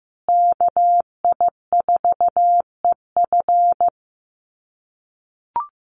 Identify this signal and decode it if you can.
Morse code, heard as a keyed tone, about 15 wpm, 700 Hz. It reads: KI4EF
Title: USA Call Signs Repeated 2x 15wpm